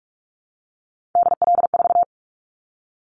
Morse code, heard as a keyed tone, about 46 wpm, 700 Hz. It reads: DL4
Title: Callsign Prefixes for Top 25 DXCC - 45wpm